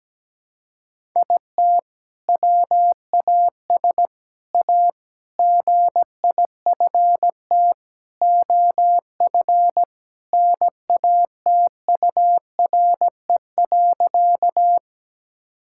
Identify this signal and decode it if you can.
Morse code, heard as a keyed tone, about 17 wpm, 700 Hz. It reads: IT WAS A GIFT OF NATURE.
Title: Sentences from Top 1000 Words - Review Part 6 - 17wpm